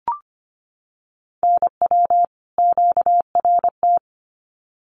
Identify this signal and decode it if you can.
Morse code, heard as a keyed tone, about 25 wpm, 700 Hz. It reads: NW QRT
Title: Common QSO Phrases 25wpm